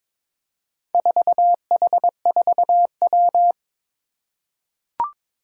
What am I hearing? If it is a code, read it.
Morse code, heard as a keyed tone, about 22 wpm, 700 Hz. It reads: 4H4W
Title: Four Challenging Letter-Number Combinations 22wpm